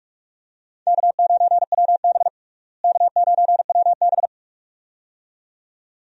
Morse code, heard as a keyed tone, about 46 wpm, 700 Hz. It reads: K9WB K9WB
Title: USA Call Signs - 4 Characters - Repeated 2x - 45wpm